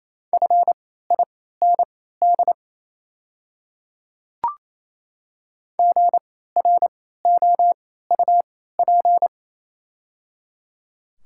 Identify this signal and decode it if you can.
Morse code, heard as a keyed tone, about 28 wpm, spaced as for 16 wpm, 700 Hz. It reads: FIND GROUP
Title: 3x Character Spacing - Top 300 Words - 28wpm